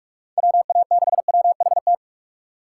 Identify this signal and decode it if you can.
Morse code, heard as a keyed tone, about 41 wpm, 700 Hz. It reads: WA6WHT
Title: USA Call Signs - 6 Characters - 4x Character Spacing - 45wpm